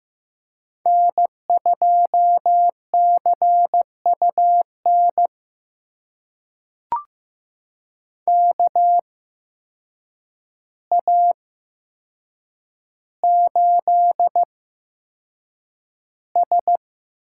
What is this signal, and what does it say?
Morse code, heard as a keyed tone, about 15 wpm, spaced as for 7 wpm, 700 Hz. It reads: N2CUN KA8S
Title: USA Call Signs - 8x Character Spacing - 15wpm